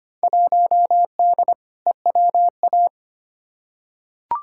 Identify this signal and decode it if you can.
Morse code, heard as a keyed tone, about 25 wpm, 700 Hz. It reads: 1D EWA
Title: Field Day Exchanges - 25wpm